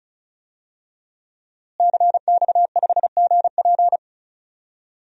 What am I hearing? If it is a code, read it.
Morse code, heard as a keyed tone, about 36 wpm, 700 Hz. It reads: CX5GP
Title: DXCC-100 Call Signs - 5 Characters - Repeated 2x - 35wpm